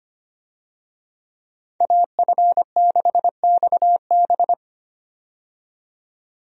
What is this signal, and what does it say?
Morse code, heard as a keyed tone, about 25 wpm, 700 Hz. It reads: AF6XB
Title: USA Call Signs - 5 Characters - 25wpm